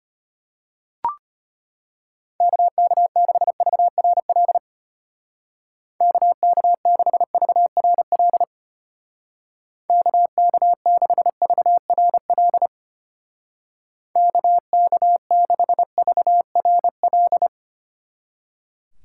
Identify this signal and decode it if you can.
Morse code, heard as a keyed tone, about 32 wpm, 700 Hz. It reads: KK6VRL KK6VRL KK6VRL KK6VRL